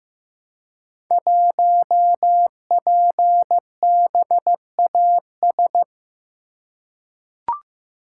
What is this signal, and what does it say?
Morse code, heard as a keyed tone, about 15 wpm, 700 Hz. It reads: 1PBAS
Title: Code Groups - 5 Letter:Numbers - 15wpm